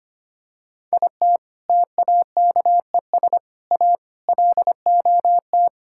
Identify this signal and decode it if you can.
Morse code, heard as a keyed tone, about 25 wpm, 700 Hz. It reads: IT TAKES A LOT
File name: practice / 25wpm / Sentences with 4-Word Phrases - 25wpm